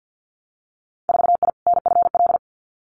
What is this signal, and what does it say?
Morse code, heard as a keyed tone, about 50 wpm, 700 Hz. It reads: 4I NFL
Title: Winter Field Day Exchanges 50wpm